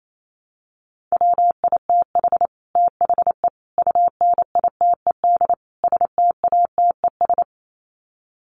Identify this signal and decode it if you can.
Morse code, heard as a keyed tone, about 28 wpm, 700 Hz. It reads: WITH THE UNITED STATES
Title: Sets of 4 Words 28wpm